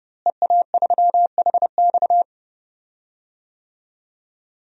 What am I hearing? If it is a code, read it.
Morse code, heard as a keyed tone, about 30 wpm, 700 Hz. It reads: EA3HX
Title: DXCC-100 Call Signs - 5 Characters - Repeated 3x - 30wpm